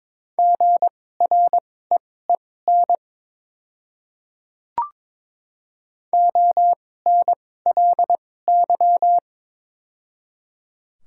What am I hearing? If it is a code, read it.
Morse code, heard as a keyed tone, about 22 wpm, spaced as for 16 wpm, 700 Hz. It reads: GREEN ONLY
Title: 2x Character Spacing - Top 500 Words - 22wpm